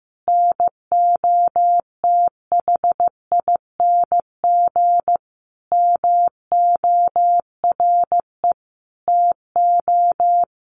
Morse code, heard as a keyed tone, about 14 wpm, 700 Hz. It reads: NOTHING MORE TO
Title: Sentences from Top 1100 Words 15wpm